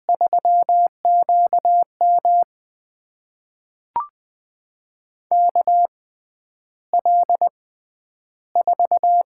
Morse code, heard as a keyed tone, about 20 wpm, spaced as for 10 wpm, 700 Hz. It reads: 3QM KL4
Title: USA Call Signs - 6x Character Spacing - 20wpm